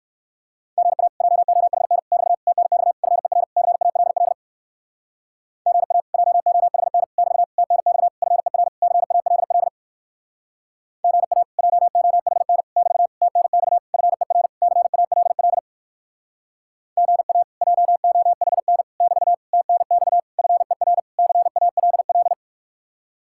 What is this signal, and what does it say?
Morse code, heard as a keyed tone, about 62 wpm, 700 Hz. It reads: GA JOHN <BT> TNX FER CALL GA JOHN <BT> TNX FER CALL GA JOHN <BT> TNX FER CALL GA JOHN <BT> TNX FER CALL